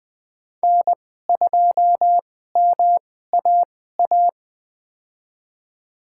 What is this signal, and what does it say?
Morse code, heard as a keyed tone, about 20 wpm, spaced as for 14 wpm, 700 Hz. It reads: N2MAA
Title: USA Call Signs - 5 Characters - 2x Character Spacing - 20wpm